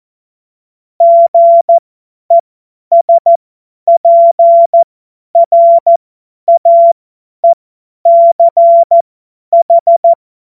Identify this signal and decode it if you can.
Morse code, heard as a keyed tone, about 14 wpm, spaced as for 10 wpm, 680 Hz. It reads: GESPRAECH
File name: Morsekurs / Witze / Witz 24. Januar 2023 50 F70